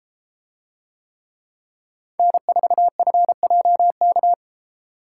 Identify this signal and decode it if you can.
Morse code, heard as a keyed tone, about 33 wpm, 700 Hz. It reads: N4FJK